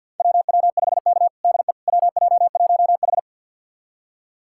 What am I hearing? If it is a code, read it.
Morse code, heard as a keyed tone, about 53 wpm, 700 Hz. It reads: WW5K DE WJ1H
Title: USA Callsign DE Callsign - 4 Characters - 50wpm